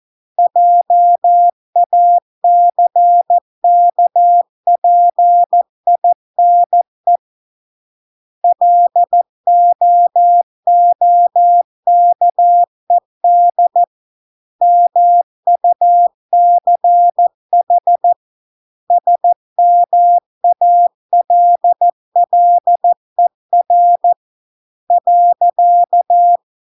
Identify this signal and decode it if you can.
Morse code, heard as a keyed tone, about 14 wpm, 700 Hz. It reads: JACKPINE LOOKED MUCH SMALLER .